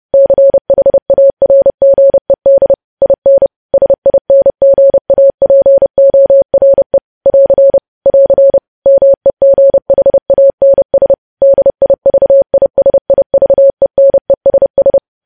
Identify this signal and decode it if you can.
Morse code, heard as a keyed tone, about 30 wpm, 550 Hz. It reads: CHARGED IN SINGAPORE <AR> <AR> MEGHANS DIVISIVENESS
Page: news headlines in morse code at 30 wpm